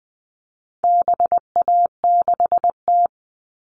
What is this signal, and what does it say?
Morse code, heard as a keyed tone, about 20 wpm, 700 Hz. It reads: BA6T
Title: All Cut Numbers - 4-Digit 20wpm